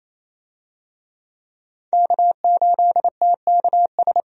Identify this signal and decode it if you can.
Morse code, heard as a keyed tone, about 28 wpm, 700 Hz. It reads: K8TKS
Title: USA Call Signs - 5 Characters - 28wpm